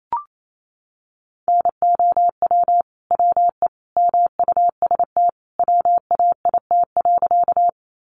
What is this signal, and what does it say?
Morse code, heard as a keyed tone, about 28 wpm, 700 Hz. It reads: NOW WE MUST WAIT.
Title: Sentences from Top 500 Words 28wpm